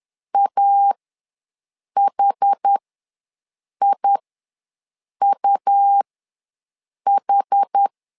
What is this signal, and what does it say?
Morse code, heard as a keyed tone, about 10 wpm, spaced as for 6 wpm, 800 Hz. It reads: AHIUH
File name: LEZIONI / 2 UHDN / 2 UHD ws 10wpm 2058